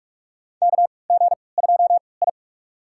Morse code, heard as a keyed tone, about 42 wpm, spaced as for 25 wpm, 700 Hz. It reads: KG2I
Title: USA Call Signs - 4 Characters - 3x Character Spacing - 45wpm